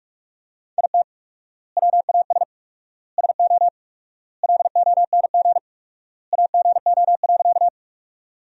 Morse code, heard as a keyed tone, about 45 wpm, 700 Hz. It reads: IT WAS SO LONG AGO.
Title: 4x Word-Spacing - Sentences from Top 500 Words - 45wpm